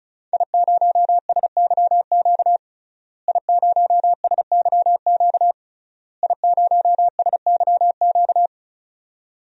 Morse code, heard as a keyed tone, about 34 wpm, 700 Hz. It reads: I0SYQ I0SYQ I0SYQ
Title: DXCC-100 Call Signs - 5 Characters - Repeated 3x - 35wpm